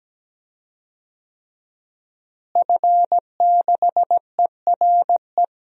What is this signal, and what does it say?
Morse code, heard as a keyed tone, about 17 wpm, 700 Hz. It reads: F6ERE
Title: DXCC-100 Call Signs - 5 Characters - Repeated 3x - 17wpm